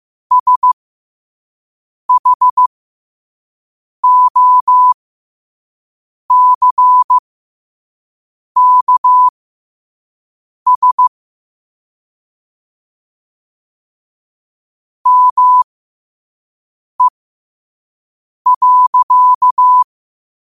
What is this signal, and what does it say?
Morse code, heard as a keyed tone, about 15 wpm, spaced as for 5 wpm, 1000 Hz. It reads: SHOCKS ME.
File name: qotdfiles / 5wpm